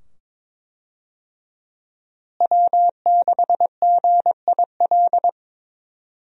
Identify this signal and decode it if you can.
Morse code, heard as a keyed tone, about 21 wpm, 700 Hz. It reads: W6GIL
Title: USA Call Signs - 5 Characters - Repeated 2x - 22wpm